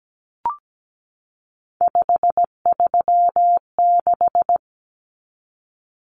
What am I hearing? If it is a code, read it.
Morse code, heard as a keyed tone, about 17 wpm, 700 Hz. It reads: 536
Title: Numbers - 3-Digit 17wpm